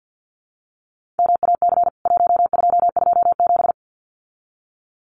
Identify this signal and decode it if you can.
Morse code, heard as a keyed tone, about 50 wpm, 700 Hz. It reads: MAC 1227
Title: CWT Exchange 50wpm